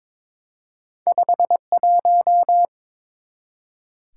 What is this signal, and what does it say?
Morse code, heard as a keyed tone, about 22 wpm, 700 Hz. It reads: 51
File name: practice / 22wpm / Two Letter-Number Combinations - 22wpm